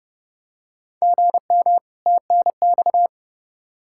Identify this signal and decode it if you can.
Morse code, heard as a keyed tone, about 32 wpm, 700 Hz. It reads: GM TNX
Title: Practice QSO - Exchange 1 - Beginning - 30wpm